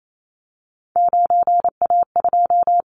Morse code, heard as a keyed tone, about 28 wpm, 700 Hz. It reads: 9A2
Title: Callsign Prefixes for Top 100 DXCC - 28wpm